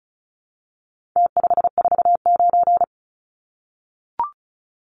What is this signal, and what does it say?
Morse code, heard as a keyed tone, about 34 wpm, 700 Hz. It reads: T549
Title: Numbers - Contest Style - 4-Digit 35wpm